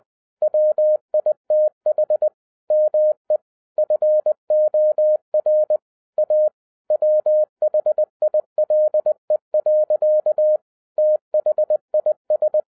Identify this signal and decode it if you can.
Morse code, heard as a keyed tone, about 20 wpm, 610 Hz. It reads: WITH ME FOR A WHILE. THIS